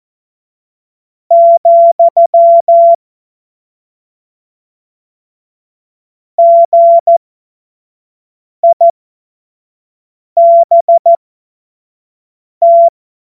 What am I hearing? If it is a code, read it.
Morse code, heard as a keyed tone, about 14 wpm, spaced as for 5 wpm, 680 Hz. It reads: , GIBT
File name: Morsekurs / Witze / Witz 25. April 2023 25 F70